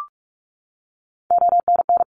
Morse code, heard as a keyed tone, about 46 wpm, 700 Hz. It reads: ONN
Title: ARRL Sections 45wpm